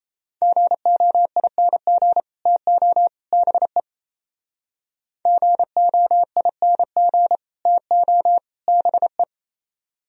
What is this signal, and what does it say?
Morse code, heard as a keyed tone, about 36 wpm, 700 Hz. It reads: GOING TO BE GOING TO BE